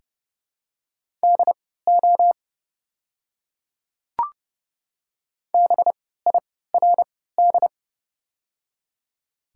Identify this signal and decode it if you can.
Morse code, heard as a keyed tone, about 30 wpm, spaced as for 17 wpm, 690 Hz. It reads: DO BIRD
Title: 3x Character Spacing - Top 500 Words - 30wpm